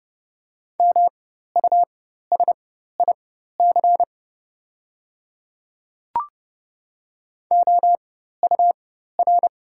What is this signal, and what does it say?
Morse code, heard as a keyed tone, about 31 wpm, spaced as for 14 wpm, 700 Hz. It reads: MUSIC OUR
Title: 4x Character Spacing - Top 300 Words - 30wpm